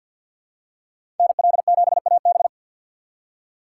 Morse code, heard as a keyed tone, about 52 wpm, 700 Hz. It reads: NL7AB
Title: USA Call Signs - 3x Character Spacing - 50wpm